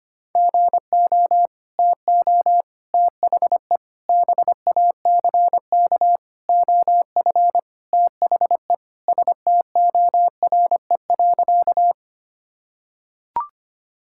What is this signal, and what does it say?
Morse code, heard as a keyed tone, about 25 wpm, 700 Hz. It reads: GO TO THE BACK OF THE STORE.
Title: Sentences from Top 1000 Words - Review Part 5 - 25wpm